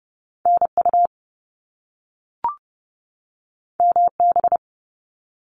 Morse code, heard as a keyed tone, about 30 wpm, 700 Hz. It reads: NU MB
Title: Canadian Provinces and Territories 30wpm